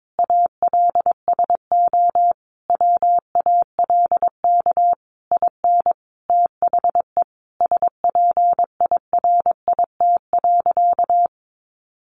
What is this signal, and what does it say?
Morse code, heard as a keyed tone, about 22 wpm, 700 Hz. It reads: ALSO WALK IN THE SPIRIT.
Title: Sentences from Top 1400 Words 22wpm